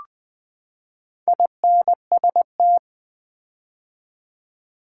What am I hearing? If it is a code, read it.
Morse code, heard as a keyed tone, about 20 wpm, 700 Hz. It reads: INST
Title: Build A Word - 4 Letters At A Time - 12-Letter Words - 20wpm